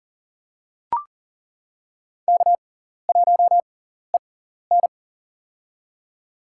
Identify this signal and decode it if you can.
Morse code, heard as a keyed tone, about 40 wpm, spaced as for 14 wpm, 700 Hz. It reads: K1EN
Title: USA Call Signs - 6x Character Spacing - 40wpm